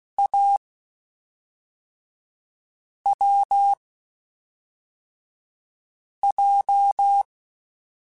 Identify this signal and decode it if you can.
Morse code, heard as a keyed tone, about 16 wpm, spaced as for 3 wpm, 800 Hz. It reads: AWJ
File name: Short Runs / Short Runs-mp3s / Unballanced @5